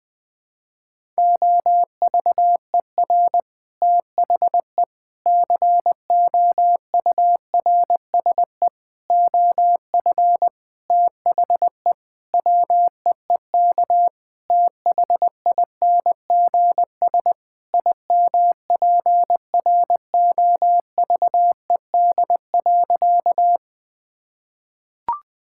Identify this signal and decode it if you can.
Morse code, heard as a keyed tone, about 20 wpm, 700 Hz. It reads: OVER THE COURSE OF THE WEEK THINGS IMPROVED.
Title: Sentences with 4-Word Phrases - Spoken Beginning - 20wpm